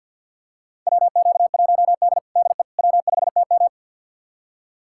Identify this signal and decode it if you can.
Morse code, heard as a keyed tone, about 53 wpm, 700 Hz. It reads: WQ1D DE W5TM
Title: USA Callsign DE Callsign - 4 Characters - 50wpm